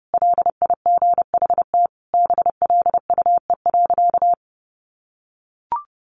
Morse code, heard as a keyed tone, about 30 wpm, 700 Hz. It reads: LIGHT BLUE.
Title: Additional Easy Phrases 30wpm